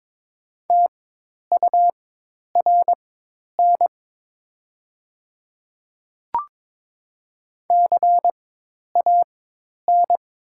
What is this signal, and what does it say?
Morse code, heard as a keyed tone, about 22 wpm, spaced as for 10 wpm, 700 Hz. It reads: TURN CAN
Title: 4x Character Spacing - Top 200 Words - 22wpm